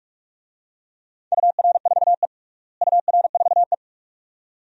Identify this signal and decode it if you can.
Morse code, heard as a keyed tone, about 46 wpm, 700 Hz. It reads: UR4E UR4E
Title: DXCC-100 Call Signs - 4 Characters - Repeated 2x - 45wpm